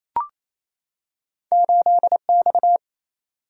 Morse code, heard as a keyed tone, about 28 wpm, 700 Hz. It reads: 8X
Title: Single-Letter Single-Number 28wpm